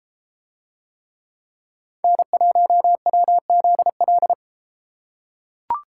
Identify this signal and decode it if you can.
Morse code, heard as a keyed tone, about 33 wpm, 700 Hz. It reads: N1WZL